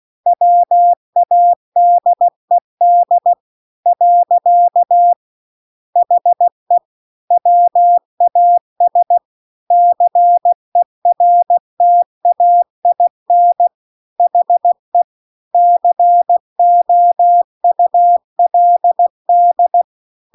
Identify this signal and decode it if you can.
Morse code, heard as a keyed tone, about 16 wpm, 700 Hz. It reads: WADED . HE WAS CERTAIN HE COULD